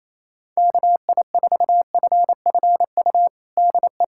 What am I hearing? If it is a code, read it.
Morse code, heard as a keyed tone, about 27 wpm, 700 Hz. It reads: KI4FFU DE
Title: USA Callsign DE Callsign 28wpm